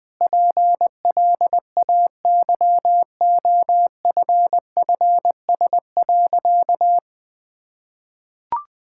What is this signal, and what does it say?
Morse code, heard as a keyed tone, about 21 wpm, 700 Hz. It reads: PLAYOFFS.
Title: Sentences with 7-Word Phrases - Spoken Beginning - 20wpm